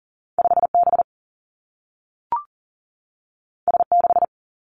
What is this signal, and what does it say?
Morse code, heard as a keyed tone, about 39 wpm, 700 Hz. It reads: 5B S6
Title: Two Challenging Letter-Number Combinations 40wpm